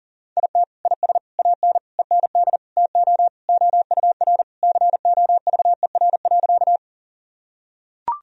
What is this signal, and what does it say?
Morse code, heard as a keyed tone, about 40 wpm, 700 Hz. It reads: IT IS AN END TO OUR COVER.
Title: Sentences from Top 1000 Words - Review Part 1 - 40wpm